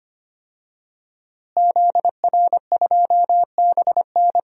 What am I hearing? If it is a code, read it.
Morse code, heard as a keyed tone, about 25 wpm, 700 Hz. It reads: ZR2BN
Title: DXCC-100 Call Signs - 5 Characters - Repeated 3x - 25wpm